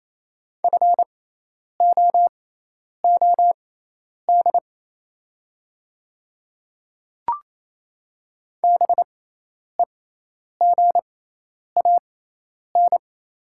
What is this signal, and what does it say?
Morse code, heard as a keyed tone, about 27 wpm, spaced as for 10 wpm, 700 Hz. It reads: FOOD BEGAN
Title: 6x Character Spacing - Top 300 Words - 28wpm